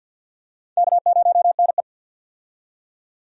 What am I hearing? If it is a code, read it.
Morse code, heard as a keyed tone, about 52 wpm, 700 Hz. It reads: K0NE
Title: USA Call Signs - 4 Characters - 50wpm